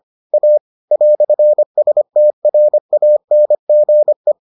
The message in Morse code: A "STRANGE

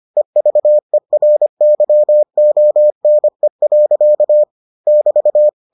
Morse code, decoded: EVERYONE. <BT>

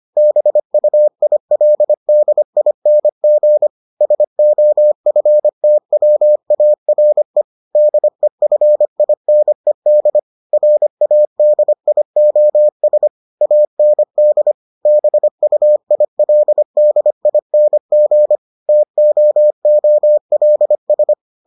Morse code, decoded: BUILDING SOFTWARE DEFINED RADIOS AND BUILDING TOOLS

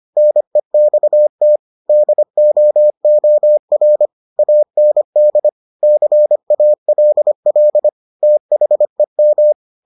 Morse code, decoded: NEXT DOOR AND CALL THEM

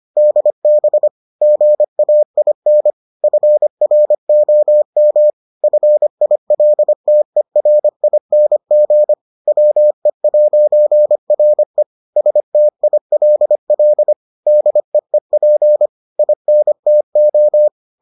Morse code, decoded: DB GAIN FROM FILTERING WE'RE STILL DEEP INTO

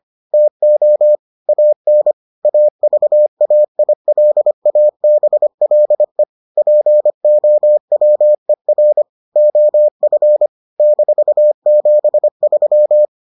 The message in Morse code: TO AN AVAILABLE POWER OF -73

